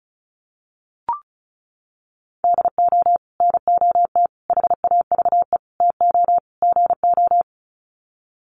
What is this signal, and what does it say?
Morse code, heard as a keyed tone, about 34 wpm, 700 Hz. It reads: DO NOT HAVE TO GO
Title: Sets of 5 Words 35wpm